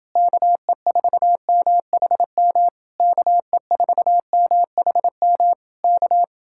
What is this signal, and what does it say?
Morse code, heard as a keyed tone, about 30 wpm, 700 Hz. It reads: KE4MHM KE4MHM K